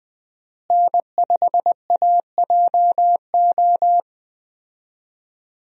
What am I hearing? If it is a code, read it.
Morse code, heard as a keyed tone, about 20 wpm, 700 Hz. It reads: N5AJO
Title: USA Call Signs - 5 Characters - Repeated 3x - 20wpm